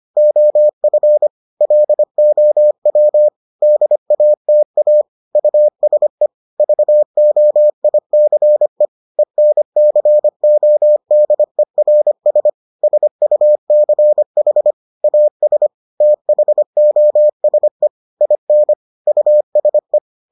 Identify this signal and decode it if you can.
Morse code, heard as a keyed tone, about 25 wpm, 600 Hz. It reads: OF LOW DATA USE VOICE ENCODERS SUCH AS THOSE IN USE